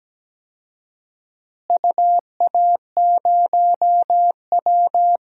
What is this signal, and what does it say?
Morse code, heard as a keyed tone, about 17 wpm, 700 Hz. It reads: UA0W
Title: DXCC-100 Call Signs - 4 Characters - Repeated 2x - 17wpm